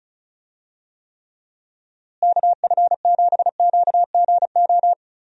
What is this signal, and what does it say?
Morse code, heard as a keyed tone, about 34 wpm, 700 Hz. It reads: KF7QGO
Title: USA Call Signs - 6 Characters - Repeated 2x - 35wpm